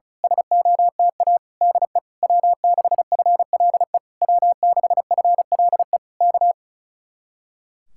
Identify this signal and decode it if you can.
Morse code, heard as a keyed tone, about 32 wpm, 700 Hz. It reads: SOTA DE W6FLE W6FLE K
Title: Speed-Racing Common QSO Phrases Encore - 35wpm